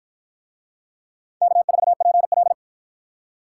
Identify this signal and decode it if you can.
Morse code, heard as a keyed tone, about 50 wpm, 700 Hz. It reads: K4PL